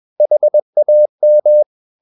Morse code, heard as a keyed tone, about 21 wpm, 600 Hz. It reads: HAM
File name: Ferie QRS 21wpm